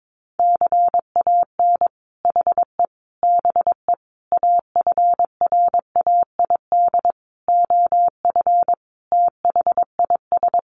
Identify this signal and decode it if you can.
Morse code, heard as a keyed tone, about 22 wpm, 700 Hz. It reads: CAN HE BE AFRAID OF THIS
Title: Sentences from Top 1500 Words 22wpm